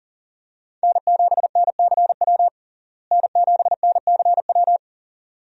Morse code, heard as a keyed tone, about 40 wpm, 700 Hz. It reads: N7NCW N7NCW
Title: USA Call Signs - 5 Characters - Repeated 2x - 40wpm